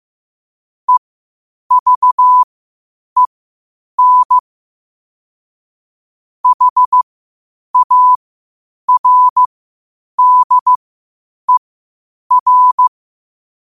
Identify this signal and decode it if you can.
Morse code, heard as a keyed tone, about 15 wpm, spaced as for 8 wpm, 1000 Hz. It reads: EVEN HARDER
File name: qotdfiles / 8wpm